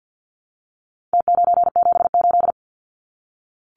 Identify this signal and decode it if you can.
Morse code, heard as a keyed tone, about 48 wpm, 700 Hz. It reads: T978
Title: Numbers - Contest Style - 4-Digit 50wpm